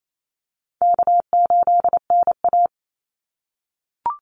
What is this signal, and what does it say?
Morse code, heard as a keyed tone, about 28 wpm, 700 Hz. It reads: K8NA
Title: Easy Call Signs - 28wpm